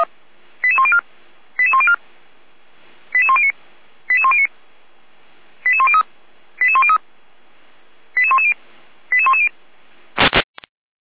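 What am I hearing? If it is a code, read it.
Five-tone selective call (ZVEI), heard as quick runs of five tones, tones of about 70 ms. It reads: 88104 88104 88109 88109 88103 88103 88110 88110
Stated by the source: Received on SDR; ZVEI five-tone call, feuerwehr-orlamuende.org (fire brigade Orlamünde).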